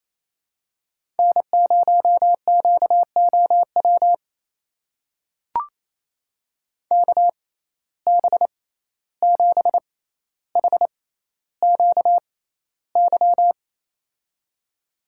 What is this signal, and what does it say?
Morse code, heard as a keyed tone, about 28 wpm, spaced as for 14 wpm, 700 Hz. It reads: N0QOW KB7HQY
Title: USA Call Signs - 6x Character Spacing - 28wpm